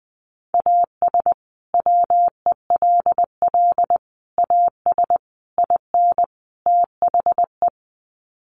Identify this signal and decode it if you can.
Morse code, heard as a keyed tone, about 20 wpm, 700 Hz. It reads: AS WELL AS IN THE
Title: Sets of 5 Words Encore 20wpm